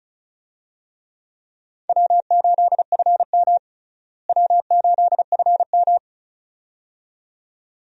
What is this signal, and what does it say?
Morse code, heard as a keyed tone, about 36 wpm, 700 Hz. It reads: W8FM W8FM
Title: USA Call Signs - 4 Characters - Repeated 2x - 35wpm